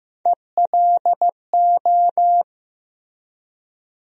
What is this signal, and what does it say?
Morse code, heard as a keyed tone, about 15 wpm, 700 Hz. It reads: ELO
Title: USA Call Sign Postfixes - 3 Letters 15wpm